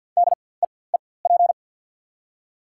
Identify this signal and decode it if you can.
Morse code, heard as a keyed tone, about 48 wpm, spaced as for 23 wpm, 700 Hz. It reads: DEEP
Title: 4x Character Spacing - Top 500 Words - 50wpm